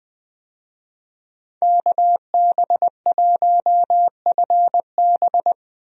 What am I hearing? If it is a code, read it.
Morse code, heard as a keyed tone, about 20 wpm, 700 Hz. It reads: KB1FB
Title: USA Call Signs - 5 Characters - 20wpm